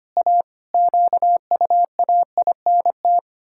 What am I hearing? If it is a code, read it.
Morse code, heard as a keyed tone, about 25 wpm, 700 Hz. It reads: A QUAINT
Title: Pangram Challenge v2 - 25wpm